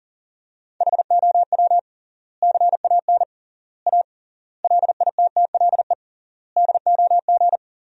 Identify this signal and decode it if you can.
Morse code, heard as a keyed tone, about 40 wpm, 700 Hz. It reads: HOW CAN A LITTLE DOG
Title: 3x Word-Spacing - Sentences from Top 500 Words - 40wpm